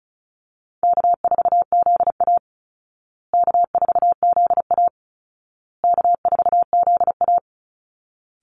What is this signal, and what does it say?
Morse code, heard as a keyed tone, about 34 wpm, 700 Hz. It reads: K4ZA K4ZA K4ZA
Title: Easy Call Signs Repeated 3x - 35wpm